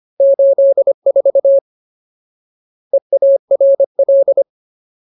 Morse code, heard as a keyed tone, about 25 wpm, 550 Hz. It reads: 84 EARL
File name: Ultimate Word List / 25WPM Ultimate